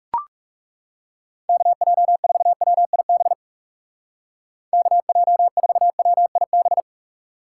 Morse code, heard as a keyed tone, about 35 wpm, 700 Hz. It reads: KJ4WIB KJ4WIB
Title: Speed Racing - USA 2x3 Call Signs 30wpm